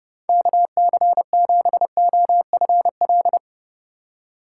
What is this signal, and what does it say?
Morse code, heard as a keyed tone, about 30 wpm, 700 Hz. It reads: KC7OFL